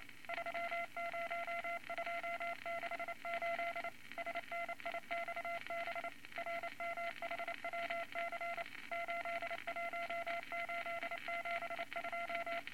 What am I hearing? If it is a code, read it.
Morse code, heard as a keyed tone, about 29 wpm, 680 Hz, with noise. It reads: 30268 SNIXB RMHWC 81872
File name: Morse Code Records / 22 A LCM 172 Twenty Words Per Minute